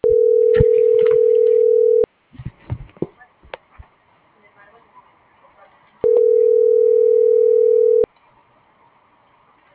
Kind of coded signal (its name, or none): telephone call progress tone